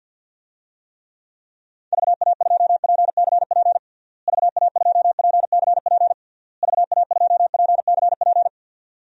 Morse code, heard as a keyed tone, about 50 wpm, 700 Hz. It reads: VA2PCP VA2PCP VA2PCP
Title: DXCC-100 Call Signs - 6 Characters - Repeated 3x - 50wpm